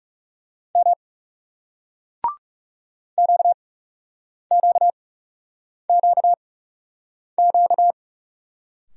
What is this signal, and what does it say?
Morse code, heard as a keyed tone, about 40 wpm, spaced as for 8 wpm, 700 Hz. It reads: M QQQQ